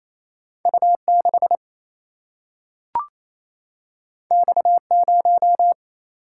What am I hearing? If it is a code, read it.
Morse code, heard as a keyed tone, about 28 wpm, 700 Hz. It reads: U6 X0
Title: Single-Letter Single-Number 28wpm